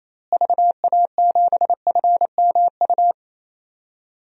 Morse code, heard as a keyed tone, about 28 wpm, 700 Hz. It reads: VA7FMU
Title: DXCC-100 Call Signs - 6 Characters - Repeated 2x - 28wpm